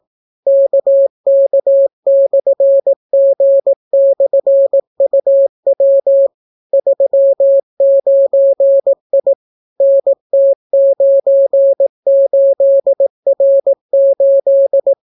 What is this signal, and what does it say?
Morse code, heard as a keyed tone, about 18 wpm, 550 Hz. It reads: KK/G/UW 39I NT98R8